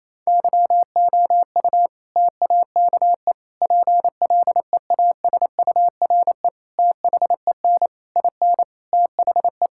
Morse code, heard as a keyed tone, about 28 wpm, 700 Hz. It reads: YOU TAKE PLEASURE THEN IN THE
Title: Sentences from Top 1600 Words 28wpm